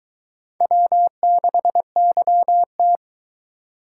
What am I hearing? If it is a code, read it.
Morse code, heard as a keyed tone, about 23 wpm, 700 Hz. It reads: W6YT